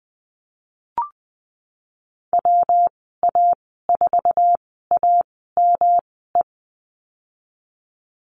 Morse code, heard as a keyed tone, about 20 wpm, spaced as for 15 wpm, 700 Hz. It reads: WA4AME
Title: 122 Lesson 20g - Number 3 - Call Signs 20wpm